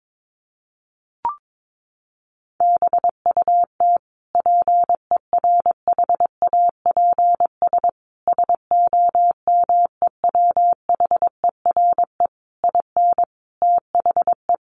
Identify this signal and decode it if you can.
Morse code, heard as a keyed tone, about 22 wpm, 700 Hz. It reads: BUT PERHAPS SOMEWHERE IN THE
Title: Sentences from Top 1400 Words 22wpm